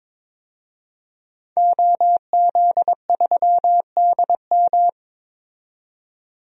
Morse code, heard as a keyed tone, about 22 wpm, 700 Hz. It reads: OZ3DM
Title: DXCC-100 Call Signs - 5 Characters - 22wpm